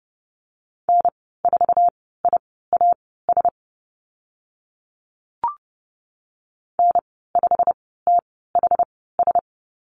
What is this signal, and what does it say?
Morse code, heard as a keyed tone, about 30 wpm, spaced as for 17 wpm, 700 Hz. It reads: N4IAS N5THS
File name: Lesson 19g - Letter w - Call Signs 30wpm